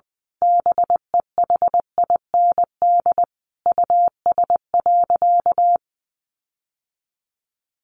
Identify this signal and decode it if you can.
Morse code, heard as a keyed tone, about 20 wpm, 700 Hz. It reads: BEHIND US.
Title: Sentences from Top 1300 Words 20wpm